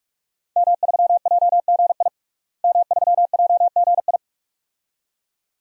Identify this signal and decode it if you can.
Morse code, heard as a keyed tone, about 42 wpm, 700 Hz. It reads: M3JGI M3JGI
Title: DXCC-100 Call Signs - 5 Characters - Repeated 2x - 45wpm